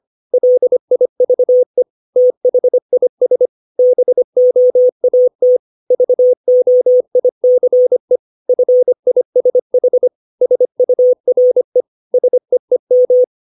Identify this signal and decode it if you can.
Morse code, heard as a keyed tone, about 25 wpm, 500 Hz. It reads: LIVE THIS BOAT VOICE FISH SURE SEEM
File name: 632 words / 632 Words 25WPM